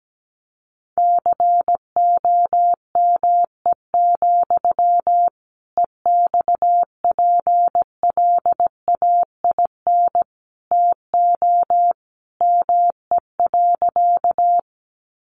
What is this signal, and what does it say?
Morse code, heard as a keyed tone, about 17 wpm, 700 Hz. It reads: COME, EXPLAIN TO ME.
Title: Sentences from Top 1100 Words 17wpm